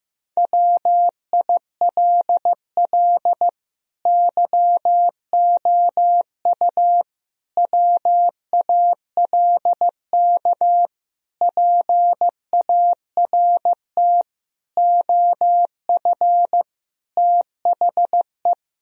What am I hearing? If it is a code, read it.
Morse code, heard as a keyed tone, about 15 wpm, 700 Hz. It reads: WILL YOU WALK PART OF THE
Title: Sentences from Top 1000 Words - Review Part 1 - 15wpm